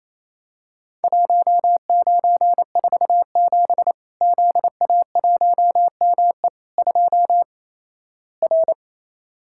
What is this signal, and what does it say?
Morse code, heard as a keyed tone, about 28 wpm, 700 Hz. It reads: 1947 ZA1ME 2 R
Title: Worked All Europe DX Contest - QTC Exchange - 28wpm